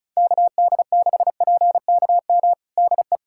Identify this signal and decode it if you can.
Morse code, heard as a keyed tone, about 34 wpm, 690 Hz. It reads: KD6PKM DE
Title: USA Callsign DE Callsign 35wpm